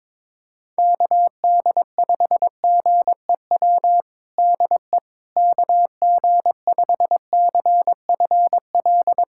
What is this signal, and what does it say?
Morse code, heard as a keyed tone, about 22 wpm, 700 Hz. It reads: KD5GEW DE KG5CFL
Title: USA Callsign DE Callsign - 6 Characters - 22wpm